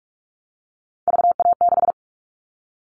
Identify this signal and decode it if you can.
Morse code, heard as a keyed tone, about 44 wpm, 700 Hz. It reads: VA6
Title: Callsign Prefixes for Top 50 DXCC - 45wpm